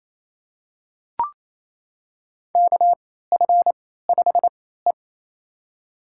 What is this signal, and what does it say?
Morse code, heard as a keyed tone, about 28 wpm, spaced as for 16 wpm, 700 Hz. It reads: KF5E
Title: USA Call Signs - 4 Characters - 3x Character Spacing - 28wpm